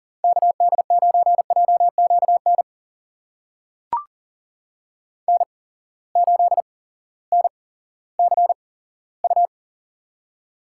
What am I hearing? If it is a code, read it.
Morse code, heard as a keyed tone, about 40 wpm, spaced as for 18 wpm, 700 Hz. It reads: KD9JQN N8NCU
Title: USA Call Signs - 8x Character Spacing - 40wpm